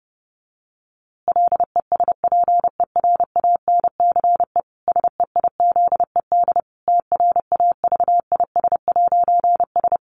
Morse code, heard as a keyed tone, about 30 wpm, 700 Hz. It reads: LESPERANCE SEIZED TRAVIS'S
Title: A Sound Of Thunder 30wpm